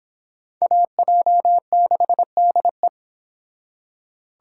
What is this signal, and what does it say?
Morse code, heard as a keyed tone, about 26 wpm, 700 Hz. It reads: AJ6DE